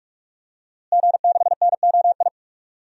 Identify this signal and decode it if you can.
Morse code, heard as a keyed tone, about 43 wpm, 700 Hz. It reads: G6NOI
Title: DXCC-100 Call Signs - 5 Characters - Repeated 3x - 45wpm